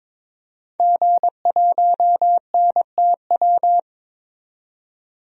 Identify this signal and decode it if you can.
Morse code, heard as a keyed tone, about 22 wpm, 700 Hz. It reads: G1NTW
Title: DXCC-100 Call Signs - 5 Characters - 22wpm